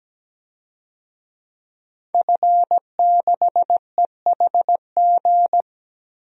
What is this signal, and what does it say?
Morse code, heard as a keyed tone, about 17 wpm, 700 Hz. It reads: F6EHG
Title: DXCC-100 Call Signs - 5 Characters - Repeated 3x - 17wpm